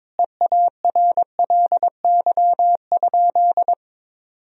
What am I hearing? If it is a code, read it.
Morse code, heard as a keyed tone, about 22 wpm, 700 Hz. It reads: EARLY?
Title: Sentences from Top 400 Words and Abbreviations 22wpm